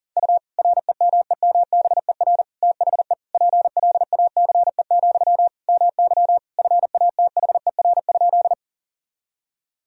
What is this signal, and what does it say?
Morse code, heard as a keyed tone, about 40 wpm, 700 Hz. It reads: U REMEMBER THE PLACE, MY FATHER?
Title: Sentences from Top 400 Words and Abbreviations 40wpm